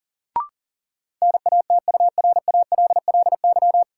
Morse code, heard as a keyed tone, about 40 wpm, 700 Hz. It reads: NATURALLY
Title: Sentences from Top 1300 Words - New Word Drill Alt - 40wpm